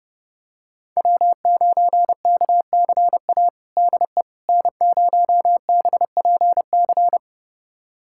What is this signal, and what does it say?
Morse code, heard as a keyed tone, about 30 wpm, 700 Hz. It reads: W9KCA DE N0BPC
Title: USA Callsign DE Callsign - 5 Characters - 30wpm